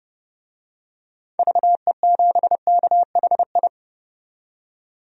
Morse code, heard as a keyed tone, about 30 wpm, 700 Hz. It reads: VE7KHI
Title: DXCC-100 Call Signs - 6 Characters - 30wpm